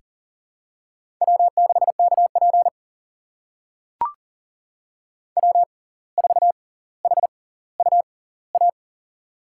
Morse code, heard as a keyed tone, about 40 wpm, spaced as for 19 wpm, 700 Hz. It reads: W6KP W4HUA
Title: USA Call Signs - 6x Character Spacing - 40wpm